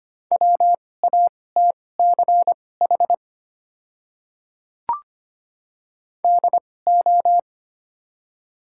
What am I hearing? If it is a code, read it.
Morse code, heard as a keyed tone, about 25 wpm, spaced as for 18 wpm, 700 Hz. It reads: WATCH DO
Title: 2x Character Spacing - Top 500 Words - 25wpm